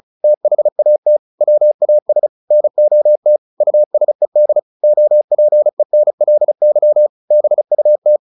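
Morse code, heard as a keyed tone, about 34 wpm, 600 Hz. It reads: THAT WAS NOT USED OPENLY BUT
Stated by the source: Morse practice at 35 wpm from CW med Gnister, sending Ken Follett's novel Never in English.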